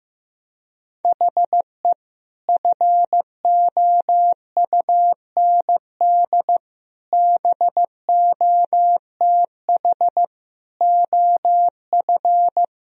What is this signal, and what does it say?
Morse code, heard as a keyed tone, about 14 wpm, 700 Hz. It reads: HE FOUND BOTH OF
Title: Sentences from Top 1000 Words - Review Part 1 - 15wpm